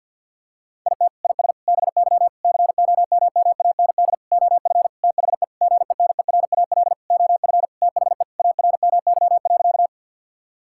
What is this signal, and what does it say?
Morse code, heard as a keyed tone, about 50 wpm, 700 Hz. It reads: IT IS BY COMMAND OF THE GENERAL OF THE ARMY.
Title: Sentences from Top 1600 Words 50wpm